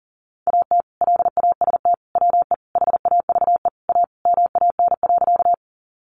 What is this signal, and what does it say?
Morse code, heard as a keyed tone, about 40 wpm, 700 Hz. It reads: AT LAST WE HAVE A MAN.
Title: Sentences from Top 300 Words 40wpm